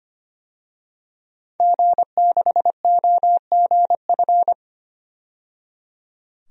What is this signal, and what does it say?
Morse code, heard as a keyed tone, about 25 wpm, 700 Hz. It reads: G6OGF
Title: DXCC-100 Call Signs - 5 Characters - 25wpm